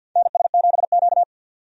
Morse code, heard as a keyed tone, about 53 wpm, 700 Hz. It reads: NS7Q
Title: USA Call Signs - 4 Characters - 50wpm